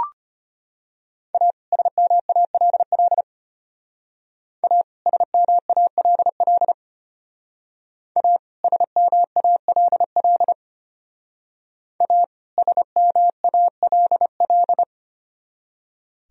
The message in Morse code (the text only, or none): A SMALL A SMALL A SMALL A SMALL